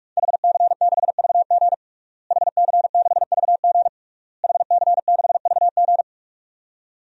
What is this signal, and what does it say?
Morse code, heard as a keyed tone, about 46 wpm, 700 Hz. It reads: HC6VG HC6VG HC6VG
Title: DXCC-100 Call Signs - 5 Characters - Repeated 3x - 45wpm